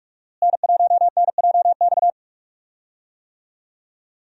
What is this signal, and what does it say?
Morse code, heard as a keyed tone, about 45 wpm, 700 Hz. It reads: N1NJX
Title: USA Call Signs - 5 Characters - Repeated 2x - 45wpm